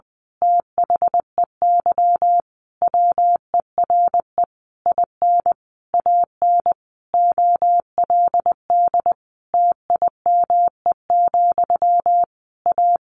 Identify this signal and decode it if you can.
Morse code, heard as a keyed tone, about 20 wpm, 700 Hz. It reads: THEY WERE IN AN OLD TIME, A
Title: A Sound Of Thunder 20wpm